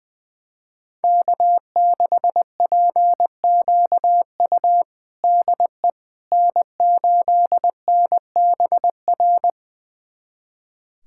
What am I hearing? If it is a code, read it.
Morse code, heard as a keyed tone, about 20 wpm, 700 Hz. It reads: K6PQU DE N8NBR
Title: USA Callsign DE Callsign - 5 Characters - 20wpm